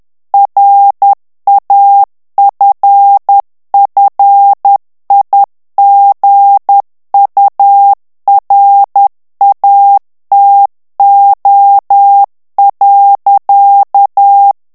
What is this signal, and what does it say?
Morse code, heard as a keyed tone, about 10 wpm, 800 Hz. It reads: RAFFIGURATO.
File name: morse mp3 / ITALIEN texte clair 1 vitesse 10 mots minute